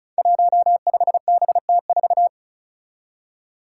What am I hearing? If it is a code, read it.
Morse code, heard as a keyed tone, about 36 wpm, 700 Hz. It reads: 15BT4
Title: Five Challenging Letter-Number Combinations 35wpm